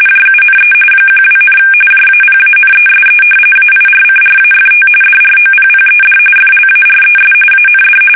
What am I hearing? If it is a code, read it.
RTTY, about 45.45 baud. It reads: JUNE 12, JAPANESE AIRCRAFT DROPPED FLARES AND
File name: fsk850 / battleshipnj